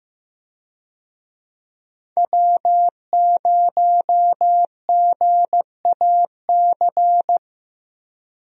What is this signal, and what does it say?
Morse code, heard as a keyed tone, about 15 wpm, 700 Hz. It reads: W0GAC
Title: USA Call Signs - 5 Characters - Repeated 2x - 15wpm